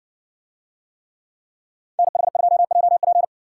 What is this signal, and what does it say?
Morse code, heard as a keyed tone, about 60 wpm, 700 Hz. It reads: NH2JP